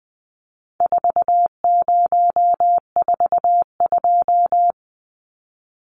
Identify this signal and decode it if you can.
Morse code, heard as a keyed tone, about 20 wpm, 700 Hz. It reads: 4042
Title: Numbers - 4-Digit 20wpm